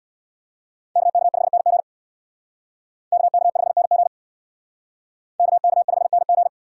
Morse code, heard as a keyed tone, about 63 wpm, 700 Hz. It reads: KK5NG KK5NG KK5NG